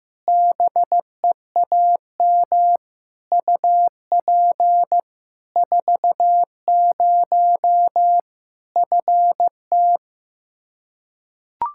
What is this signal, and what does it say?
Morse code, heard as a keyed tone, about 15 wpm, 700 Hz. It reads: BEAM UP 40 FT
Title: K1USN SST QSO Practice 15wpm